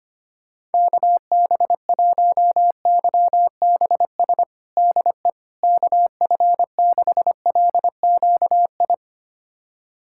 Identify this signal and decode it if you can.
Morse code, heard as a keyed tone, about 25 wpm, 700 Hz. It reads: KB1YBS DE KF6LQI
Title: USA Callsign DE Callsign - 6 Characters - 25wpm